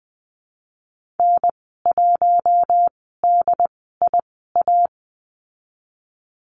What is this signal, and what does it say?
Morse code, heard as a keyed tone, about 20 wpm, spaced as for 15 wpm, 700 Hz. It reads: N1DIA
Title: 069 Lesson 12g - Letter d - Call Signs 20wpm